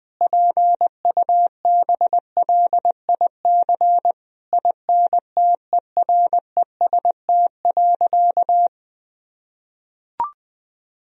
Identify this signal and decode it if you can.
Morse code, heard as a keyed tone, about 20 wpm, 700 Hz. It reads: PUBLIC INTEREST.
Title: Sentences from Top 1000 Words - Review Part 4 - 20wpm